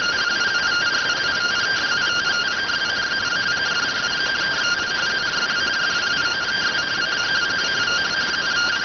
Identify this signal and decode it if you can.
Two-tone teleprinter signal, about 170 Hz shift.